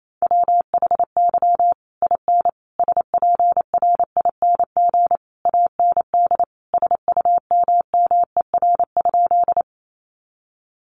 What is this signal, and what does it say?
Morse code, heard as a keyed tone, about 28 wpm, 700 Hz. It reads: WHY IN SPRING AND SUMMER?
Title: Sentences from Top 1200 Words 28wpm